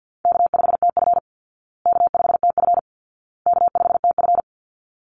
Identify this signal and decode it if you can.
Morse code, heard as a keyed tone, about 50 wpm, 700 Hz. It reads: K5TF K5TF K5TF
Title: Call Signs Repeated 2x - 50wpm